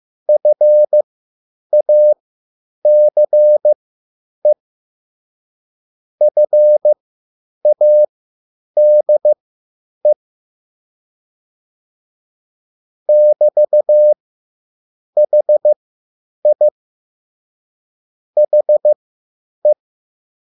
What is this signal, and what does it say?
Morse code, heard as a keyed tone, about 15 wpm, spaced as for 8 wpm, 600 Hz. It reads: FACE FADE <BT>HI HE
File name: QSL / CW lesson